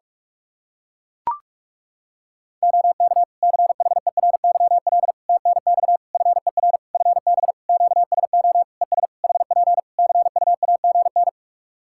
Morse code, heard as a keyed tone, about 42 wpm, 700 Hz. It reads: OK CHERYL TNX FER FB QSO ES HP CUAGN
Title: Practice QSO - Exchange 4 - Ending - 45wpm